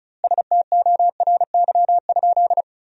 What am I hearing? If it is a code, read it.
Morse code, heard as a keyed tone, about 34 wpm, 700 Hz. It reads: STORY?
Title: 2x Word-Spacing - Sentences from Top 400 Words - 35wpm